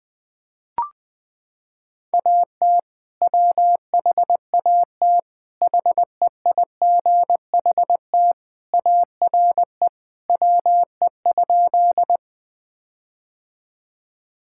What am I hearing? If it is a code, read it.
Morse code, heard as a keyed tone, about 19 wpm, 700 Hz. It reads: AT WHAT HEIGHT ARE WE?
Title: Sentences from Top 1600 Words 20wpm